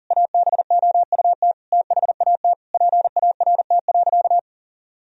Morse code, heard as a keyed tone, about 40 wpm, 700 Hz. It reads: ABOUT THAT PART.
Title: Sentences from Top 1000 Words - Review Part 1 - 40wpm